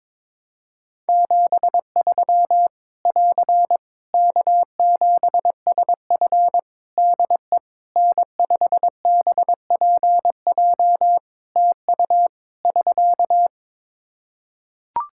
Morse code, heard as a keyed tone, about 22 wpm, 700 Hz. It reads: 73 <AR> K7SF DE N5BPJ TU <SK>
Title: Practice QSO - Exchange 4 - Ending - 22wpm